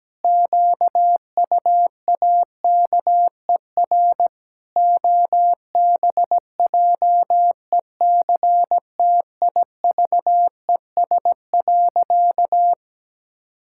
Morse code, heard as a keyed tone, about 17 wpm, 700 Hz. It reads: QUAKER OBJECTIVES.
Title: Pangram Challenge v2 - 17wpm